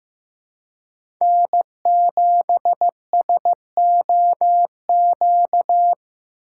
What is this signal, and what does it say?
Morse code, heard as a keyed tone, about 15 wpm, 700 Hz. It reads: N7SOQ
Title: USA Call Signs - 4x Character Spacing - 15wpm